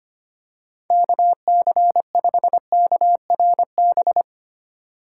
Morse code, heard as a keyed tone, about 25 wpm, 700 Hz. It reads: KC5KRB
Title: USA Call Signs - 6 Characters - 2x Character Spacing - 25wpm